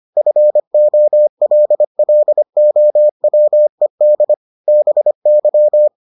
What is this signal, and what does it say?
Morse code, heard as a keyed tone, about 25 wpm, 600 Hz. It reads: FOLLOWED BY